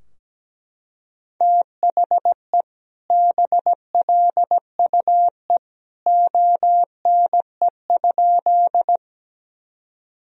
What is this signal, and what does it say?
Morse code, heard as a keyed tone, about 17 wpm, 700 Hz. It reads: THE BLUE ONE?
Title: Sentences from Top 1000 Words - Review Part 3 - 17wpm